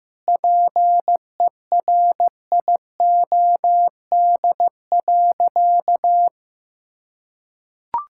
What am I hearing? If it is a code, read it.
Morse code, heard as a keyed tone, about 15 wpm, 700 Hz. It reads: PERIOD.
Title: Sentences from Top 1000 Words - Review Part 5 - 15wpm